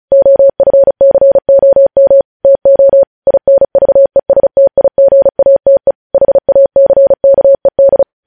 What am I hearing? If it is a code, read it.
Morse code, heard as a keyed tone, about 36 wpm, 550 Hz. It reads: OFCOM TO INVESTIGATE HACKED
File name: podcast / News headlines-1x-35x35